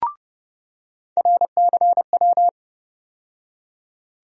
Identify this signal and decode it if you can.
Morse code, heard as a keyed tone, about 30 wpm, 700 Hz. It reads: RCW
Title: USA Call Sign Postfixes - 3 Letters 30wpm